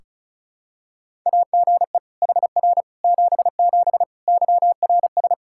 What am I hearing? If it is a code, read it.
Morse code, heard as a keyed tone, about 36 wpm, 700 Hz. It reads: AGE HR 77 YRS
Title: Practice QSO - Exchange 3 - More Info - 35wpm